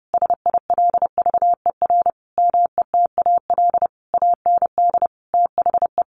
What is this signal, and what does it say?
Morse code, heard as a keyed tone, about 30 wpm, 700 Hz. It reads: SILVER METAL AND THE
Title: A Sound Of Thunder 30wpm